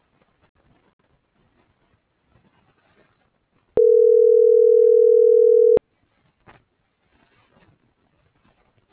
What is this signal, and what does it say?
Ringback tone (North America), 440 + 480 Hz.